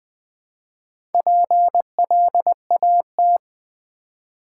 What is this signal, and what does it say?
Morse code, heard as a keyed tone, about 20 wpm, 700 Hz. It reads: PLAT
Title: Build A Word - 1 Letter At A Time - 5-Letter Words - 20wpm